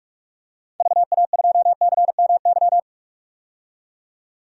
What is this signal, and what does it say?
Morse code, heard as a keyed tone, about 46 wpm, 700 Hz. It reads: VA2CMY
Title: DXCC-100 Call Signs - 6 Characters - 45wpm